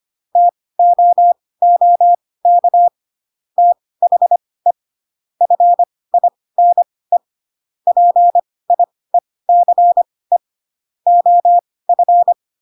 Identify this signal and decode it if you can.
Morse code, heard as a keyed tone, about 25 wpm, spaced as for 18 wpm, 700 Hz. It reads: TOOK THE FINE PIECE OF
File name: Pinocchio / Pinocchio ch2 12 25 ik1hge